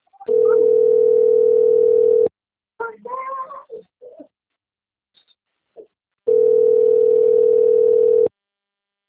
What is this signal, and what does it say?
Ringback tone (North America), 440 + 480 Hz, with speech.
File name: rg-1600-unknown- - - .55359